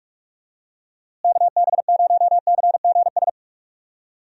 Morse code, heard as a keyed tone, about 46 wpm, 700 Hz. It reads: KB0CGS
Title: USA Call Signs - 6 Characters - Repeated 2x - 45wpm